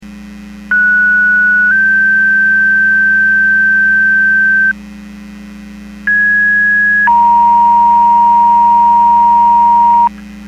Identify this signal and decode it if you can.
Two-tone page, heard as a one-second tone, then a three-second tone, about 1470 then 1550 Hz, about 1650 then 940 Hz. It is heard from 0.7 to 10.1 s.
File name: Portage 154.385 Paging